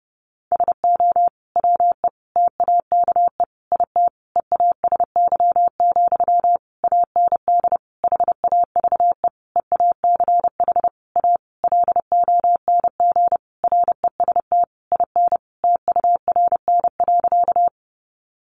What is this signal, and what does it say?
Morse code, heard as a keyed tone, about 30 wpm, 700 Hz. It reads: SO WE TAKE IT EASY, AND HAVE EACH A LONG REST IN TURN.
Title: Sentences from Top 700 Words 30wpm